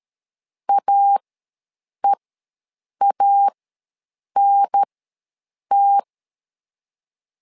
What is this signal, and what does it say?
Morse code, heard as a keyed tone, about 13 wpm, spaced as for 7 wpm, 800 Hz. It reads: AEANT
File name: LEZIONI / 2 UHDN / 2 UHDN ws 12wpm 1708